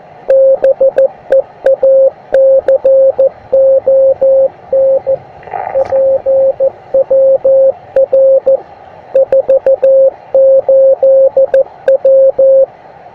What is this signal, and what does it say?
Morse code, heard as a keyed tone, about 15 wpm, 540 Hz. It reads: BEACON PWR 48W
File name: attachments / dl0igi beacon 10m